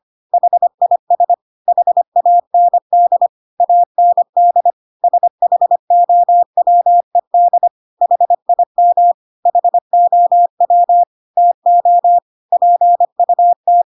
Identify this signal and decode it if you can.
Morse code, heard as a keyed tone, about 25 wpm, 700 Hz. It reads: HIS HAND AND SHOWED HIM HOW TO PUT
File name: Pinocchio / Pinocchio ch3 25 25 ik1hge